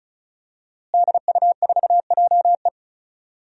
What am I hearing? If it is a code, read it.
Morse code, heard as a keyed tone, about 34 wpm, 700 Hz. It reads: DU4JE
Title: DXCC-100 Call Signs - 5 Characters - Repeated 3x - 35wpm